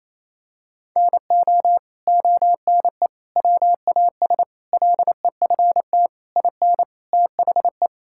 Morse code, heard as a keyed tone, about 28 wpm, 700 Hz. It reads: NO ONE WAS LEFT IN THE
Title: Sentences from Top 1000 Words - Review Part 5 - 28wpm